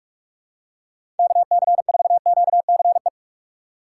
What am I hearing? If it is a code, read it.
Morse code, heard as a keyed tone, about 42 wpm, 690 Hz. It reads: KC4QCE
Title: USA Call Signs - 6 Characters - 2x Character Spacing - 45wpm